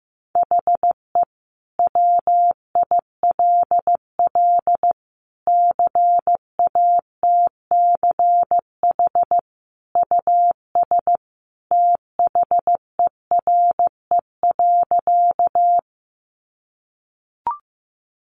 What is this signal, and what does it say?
Morse code, heard as a keyed tone, about 15 wpm, 700 Hz. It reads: HE WILL CATCH US THERE.
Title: Sentences from Top 1500 Words 15wpm